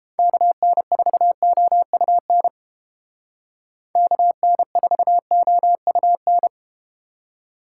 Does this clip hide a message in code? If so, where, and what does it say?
Yes: Morse code from 0.2 to 6.5 s, heard as a keyed tone, about 31 wpm, 700 Hz. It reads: KN4OUN KN4OUN